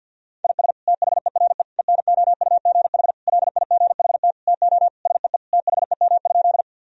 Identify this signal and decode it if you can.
Morse code, heard as a keyed tone, about 46 wpm, 700 Hz. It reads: IS THERE ENOUGH LIGHT TO SEE THEM?
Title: Sentences from Top 1000 Words - Review Part 1 - 50wpm